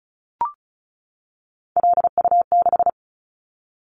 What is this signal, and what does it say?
Morse code, heard as a keyed tone, about 36 wpm, 700 Hz. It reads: LU6
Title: Callsign Prefixes for Top 50 DXCC - 35wpm